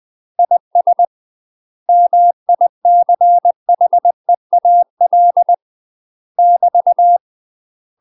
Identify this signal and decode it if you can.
Morse code, heard as a keyed tone, about 20 wpm, 700 Hz. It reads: IS MICHEAL <BT>
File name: UK MASTER QSOs TRAINING / 20WPM / UK MASTER QSOs 0022